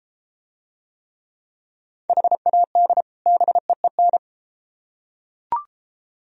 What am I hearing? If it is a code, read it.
Morse code, heard as a keyed tone, about 32 wpm, 700 Hz. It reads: HAD BEEN